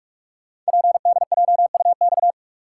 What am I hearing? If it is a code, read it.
Morse code, heard as a keyed tone, about 44 wpm, 690 Hz. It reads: PDJUX
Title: Code Groups - 5 Letters - 45wpm